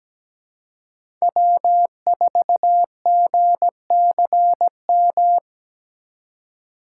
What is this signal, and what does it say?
Morse code, heard as a keyed tone, about 17 wpm, 700 Hz. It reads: W4GCM
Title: USA Call Signs - 5 Characters - 2x Character Spacing - 17wpm